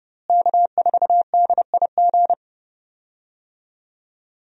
Morse code, heard as a keyed tone, about 30 wpm, 700 Hz. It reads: K4DIG
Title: USA Call Signs - 5 Characters - Repeated 2x - 30wpm